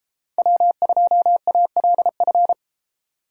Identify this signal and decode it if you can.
Morse code, heard as a keyed tone, about 30 wpm, 700 Hz. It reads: W2ALF